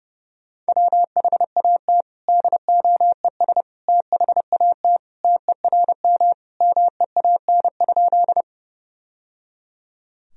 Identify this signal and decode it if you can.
Morse code, heard as a keyed tone, about 30 wpm, 700 Hz. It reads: WHAT DOES THAT TERM MEAN?
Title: Sentences from Top 1000 Words - Review Part 4 - 30wpm